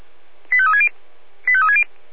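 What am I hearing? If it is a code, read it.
Five-tone selective call (ZVEI), heard as quick runs of five tones, tones of about 70 ms. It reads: 85360 85360
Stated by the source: ZVEI five-tone selective call received on an SDR.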